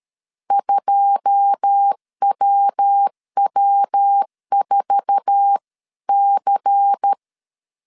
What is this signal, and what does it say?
Morse code, heard as a keyed tone, about 13 wpm, 800 Hz. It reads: 2WW4C
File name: CW / DET / 1 9 1500C S B 12wpm 3819